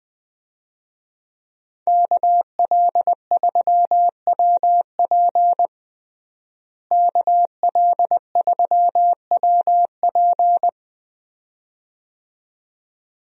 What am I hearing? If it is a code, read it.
Morse code, heard as a keyed tone, about 20 wpm, 700 Hz. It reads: KL3WP KL3WP
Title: USA Call Signs - 5 Characters - Repeated 2x - 20wpm